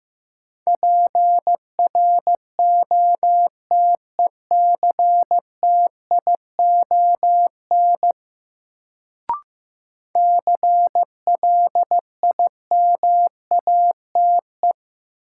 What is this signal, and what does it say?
Morse code, heard as a keyed tone, about 15 wpm, 700 Hz. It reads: PROTECTION CLIMATE
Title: Sentences from Top 1600 Words - New Word Drill - 15wpm